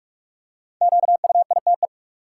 Morse code, heard as a keyed tone, about 46 wpm, 700 Hz. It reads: QUITE
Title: Five Letter Words - 45wpm